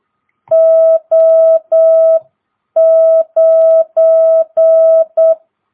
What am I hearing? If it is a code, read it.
Morse code, heard as a keyed tone, about 8 wpm, 660 Hz. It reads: O9
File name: jh8cbh / 2022cw